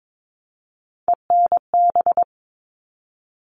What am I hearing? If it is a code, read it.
Morse code, heard as a keyed tone, about 22 wpm, 700 Hz. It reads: ENB
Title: All Cut Numbers - 3-Digit 22wpm